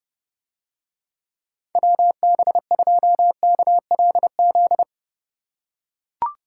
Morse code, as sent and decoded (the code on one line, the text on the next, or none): .-- -... ..--- -.- .-.. --..
WB2KLZ